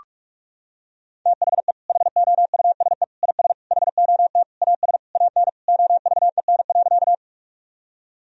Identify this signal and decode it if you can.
Morse code, heard as a keyed tone, about 41 wpm, 700 Hz. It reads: THE HOUSE IS HOT AS AN OVEN.
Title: Sentences from Top 1000 Words - Review Part 4 - 45wpm